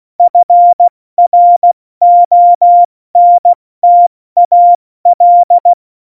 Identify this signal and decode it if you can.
Morse code, heard as a keyed tone, about 16 wpm, 700 Hz. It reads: FRONTAL
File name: Morsekurs / Witze / Witz 01. Dezember 2020, 70 F80